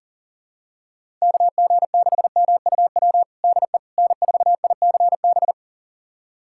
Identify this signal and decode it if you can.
Morse code, heard as a keyed tone, about 40 wpm, 700 Hz. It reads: KG6MUW DE N4ICB
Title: USA Callsign DE Callsign 40wpm